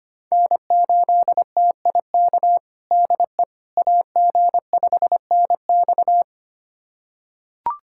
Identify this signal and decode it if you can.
Morse code, heard as a keyed tone, about 25 wpm, 700 Hz. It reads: N8TIK DE AG5NX
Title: USA Callsign DE Callsign - 5 Characters - 25wpm